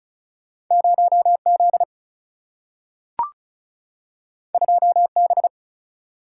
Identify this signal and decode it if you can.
Morse code, heard as a keyed tone, about 36 wpm, 700 Hz. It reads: 0 Z 2 B
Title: Single-Letter Single-Number 35wpm